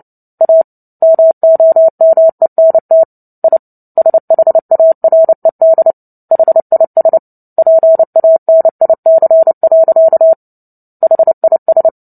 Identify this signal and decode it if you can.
Morse code, heard as a keyed tone, about 30 wpm, 650 Hz. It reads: A MOMENT I SHARED HIS PANIC. HIS
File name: Book Two / 03-Chapter 3 - 29 WPM